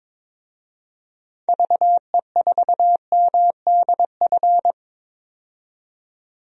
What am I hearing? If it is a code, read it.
Morse code, heard as a keyed tone, about 22 wpm, 700 Hz. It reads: VE4MDF
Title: DXCC-100 Call Signs - 6 Characters - 22wpm